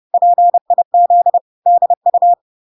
Morse code, heard as a keyed tone, about 30 wpm, 700 Hz. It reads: PIZDU